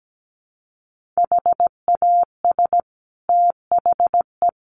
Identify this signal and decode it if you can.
Morse code, heard as a keyed tone, about 17 wpm, 700 Hz. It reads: HAS THE
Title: Sentences from Top 1500 Words 17wpm